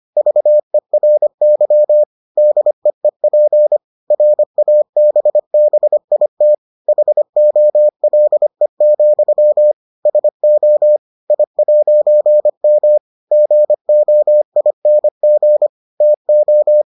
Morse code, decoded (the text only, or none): VERY DEEP RABBIT HOLE, SO I'M GOING TO